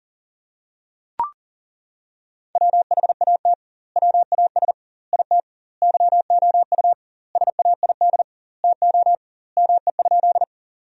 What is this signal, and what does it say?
Morse code, heard as a keyed tone, about 40 wpm, 700 Hz. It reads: WHAT WAS IT YOU SAID TO ME?
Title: 2x Word-Spacing - Sentences from Top 200 Words - 40wpm